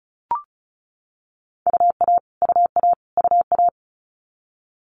Morse code, heard as a keyed tone, about 36 wpm, 700 Hz. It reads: UA UA UA
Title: Multi-letter Phonemes Repeated 3x 35wpm